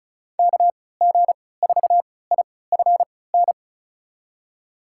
Morse code, heard as a keyed tone, about 36 wpm, spaced as for 20 wpm, 700 Hz. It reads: KG4IFN
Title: USA Call Signs - 3x Character Spacing - 35wpm